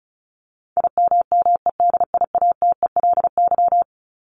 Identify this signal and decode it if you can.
Morse code, heard as a keyed tone, about 34 wpm, 700 Hz. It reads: IMMEDIATELY
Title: LY Suffix - Top 100 Words with Base Words 35wpm